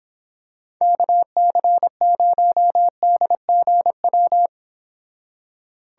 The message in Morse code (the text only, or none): KC0DGW